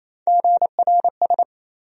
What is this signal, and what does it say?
Morse code, heard as a keyed tone, about 28 wpm, 700 Hz. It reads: GRS
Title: USA Call Sign Postfixes - 3 Letters 28wpm